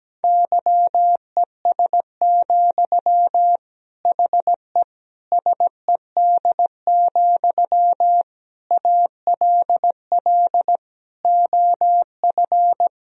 Morse code, heard as a keyed tone, about 17 wpm, 700 Hz. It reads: YES, HE SED, ALL OF
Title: Sentences from Top 500 Words and Abbreviations 17wpm